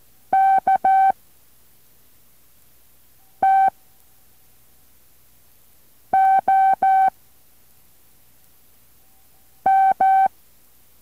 Morse code, heard as a keyed tone, about 14 wpm, 770 Hz. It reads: K T O M